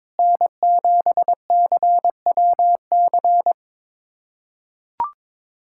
Morse code, heard as a keyed tone, about 22 wpm, 700 Hz. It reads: N7CWC
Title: USA Callsign DE Callsign - 5 Characters - 22wpm